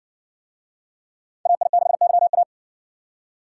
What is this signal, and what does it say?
Morse code, heard as a keyed tone, about 60 wpm, 700 Hz. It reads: AI6YA